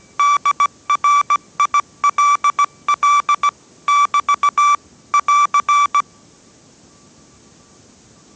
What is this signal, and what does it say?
Morse code, heard as a keyed tone, about 17 wpm, 1160 Hz. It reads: DRILL <BT> <AR>
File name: 18wpm